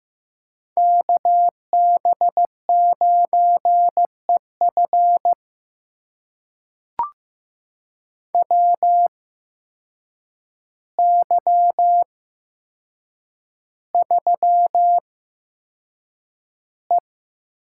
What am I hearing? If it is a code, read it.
Morse code, heard as a keyed tone, about 15 wpm, spaced as for 7 wpm, 700 Hz. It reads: KB9EF WY3E
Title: USA Call Signs - 8x Character Spacing - 15wpm